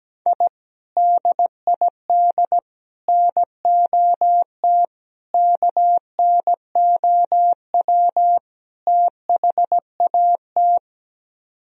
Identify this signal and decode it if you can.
Morse code, heard as a keyed tone, about 17 wpm, 700 Hz. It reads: I DID NOT KNOW THAT
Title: Speed-Racing - Sets of 5 Words 17wpm